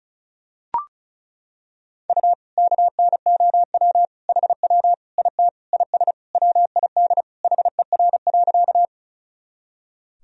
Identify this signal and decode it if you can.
Morse code, heard as a keyed tone, about 34 wpm, 700 Hz. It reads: U KNOW HW IT IS WID HER.
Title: Sentences from Top 100 Words and Abbreviations 35wpm